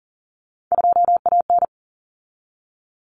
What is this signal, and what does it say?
Morse code, heard as a keyed tone, about 40 wpm, 700 Hz. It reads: 2AN
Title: Common Cut Numbers - 3-Digit 40wpm